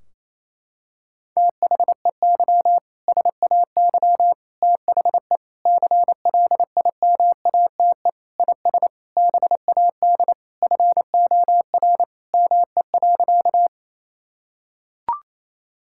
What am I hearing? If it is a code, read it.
Morse code, heard as a keyed tone, about 28 wpm, 700 Hz. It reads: THEY SAY THE CLIMATE IS BAD FOR ME.
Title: Sentences from Top 1600 Words 28wpm